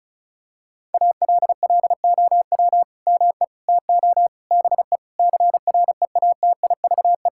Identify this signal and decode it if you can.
Morse code, heard as a keyed tone, about 36 wpm, 700 Hz. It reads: ALLOW ME TO BE CREATIVE
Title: Sentences from Top 1000 Words - Review Part 5 - 35wpm